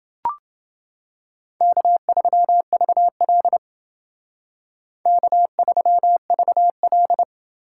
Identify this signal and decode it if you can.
Morse code, heard as a keyed tone, about 28 wpm, 700 Hz. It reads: K3VL K3VL